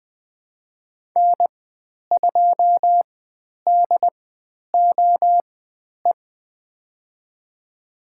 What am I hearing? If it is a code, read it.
Morse code, heard as a keyed tone, about 20 wpm, spaced as for 10 wpm, 700 Hz. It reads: N2DOE
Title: 095 Lesson 16g - Letter c - Call Signs 20wpm